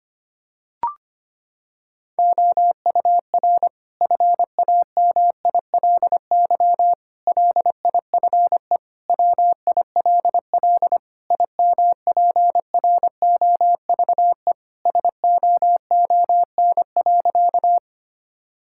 Morse code, heard as a keyed tone, about 25 wpm, 700 Hz. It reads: OUR FAMILY LIFE WILL IMPROVE SOON.
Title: Sentences from Top 1000 Words - Review Part 5 - 25wpm